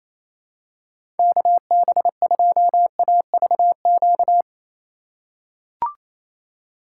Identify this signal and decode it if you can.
Morse code, heard as a keyed tone, about 28 wpm, 700 Hz. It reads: KB2AVQ
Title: USA Call Signs - 6 Characters - 2x Character Spacing - 28wpm